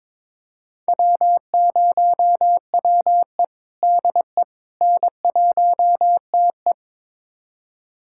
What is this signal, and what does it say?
Morse code, heard as a keyed tone, about 22 wpm, 700 Hz. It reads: W0WE DE N1TE
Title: USA Callsign DE Callsign - 4 Characters - 22wpm